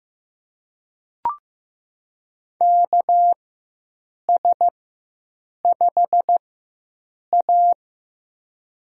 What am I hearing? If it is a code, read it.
Morse code, heard as a keyed tone, about 15 wpm, spaced as for 7 wpm, 700 Hz. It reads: KS5A
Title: USA Call Signs - 4 Characters - 4x Character Spacing - 15wpm